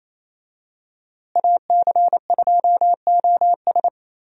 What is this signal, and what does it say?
Morse code, heard as a keyed tone, about 28 wpm, 700 Hz. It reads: AC2OS
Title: USA Call Signs - 5 Characters - Repeated 2x - 28wpm